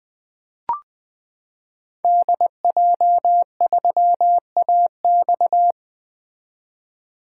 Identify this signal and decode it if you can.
Morse code, heard as a keyed tone, about 20 wpm, 700 Hz. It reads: DJ3AX
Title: 2023 SOTA Activator Call Signs - 20wpm